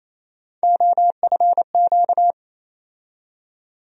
Morse code, heard as a keyed tone, about 27 wpm, 700 Hz. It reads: OFQ
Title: USA Call Sign Postfixes - 3 Letters 28wpm